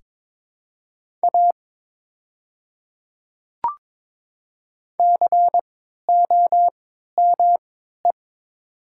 Morse code, heard as a keyed tone, about 22 wpm, spaced as for 12 wpm, 700 Hz. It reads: A COME
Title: 3x Character Spacing - Top 200 Words - 22wpm